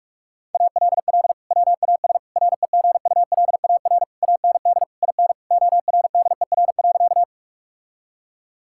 Morse code, heard as a keyed tone, about 44 wpm, 700 Hz. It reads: ALL WAS REGULAR AND IN ORDER.
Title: Sentences from Top 1000 Words - Review Part 6 - 45wpm